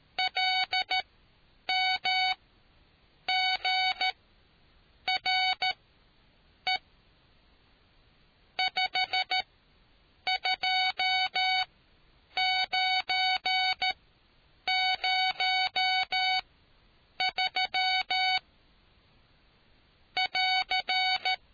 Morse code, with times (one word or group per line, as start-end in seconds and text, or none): 0.2-6.8 LMGRE
8.6-18.4 52903
20.2-21.3 <AR>